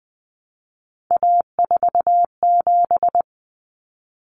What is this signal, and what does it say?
Morse code, heard as a keyed tone, about 20 wpm, 700 Hz. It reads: A47
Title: Common Cut Numbers - 3-Digit 20wpm